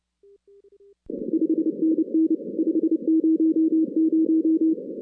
Morse code, heard as a keyed tone, about 29 wpm, 320 Hz. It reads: 5NN 500
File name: 2016 arrl dx cw / 2016-ARRL-DX-CW-K5TR- - 0-41